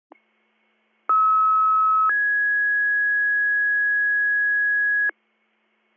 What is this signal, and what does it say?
Two-tone page, heard as a one-second tone, then a three-second tone, about 1290 then 1650 Hz, with noise. It is heard from 1.1 to 5.1 s.